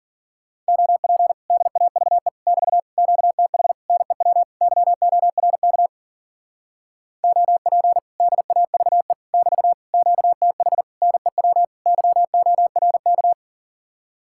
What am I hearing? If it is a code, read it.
Morse code, heard as a keyed tone, about 48 wpm, 700 Hz. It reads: OP DAVE <BT> QTH NEW YORK OP DAVE <BT> QTH NEW YORK
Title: Speed-Racing Common QSO Phrases - 40wpm